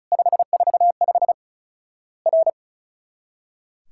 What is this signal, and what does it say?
Morse code, heard as a keyed tone, about 36 wpm, 700 Hz. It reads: 545 R
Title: Worked All Europe DX Contest - QTC Exchange - 35wpm